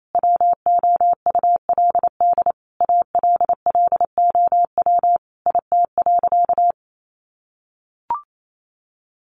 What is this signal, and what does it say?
Morse code, heard as a keyed tone, about 28 wpm, 700 Hz. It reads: WOULD ALLOW IT.
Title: Sentences from Top 700 Words 28wpm